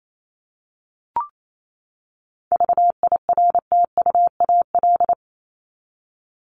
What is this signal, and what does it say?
Morse code, heard as a keyed tone, about 28 wpm, 700 Hz. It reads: VIRTUAL
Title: LY Suffix - Top 100 Words with Base Words 28wpm